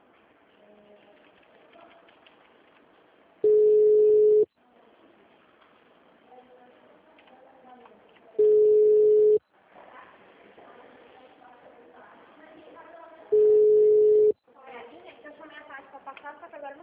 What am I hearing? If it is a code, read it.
Ringback tone (Europe, 425 Hz), with speech.